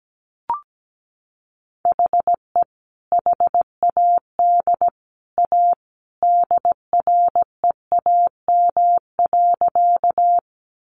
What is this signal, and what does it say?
Morse code, heard as a keyed tone, about 17 wpm, 700 Hz. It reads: HE HAD A DREAM.
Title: Sentences from Top 1500 Words 17wpm